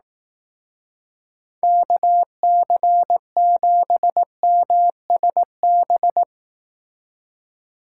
Morse code, heard as a keyed tone, about 18 wpm, 700 Hz. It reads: KC7MSB
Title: Speed Racing - USA 2x3 Call Signs 15wpm